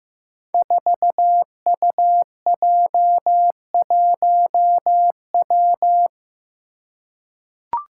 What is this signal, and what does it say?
Morse code, heard as a keyed tone, about 15 wpm, 700 Hz. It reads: 4UJ1W
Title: Five Challenging Letter-Number Combinations 15wpm